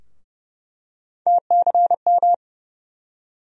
Morse code, heard as a keyed tone, about 30 wpm, 700 Hz. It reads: TCM
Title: USA Call Sign Postfixes - 3 Letters 30wpm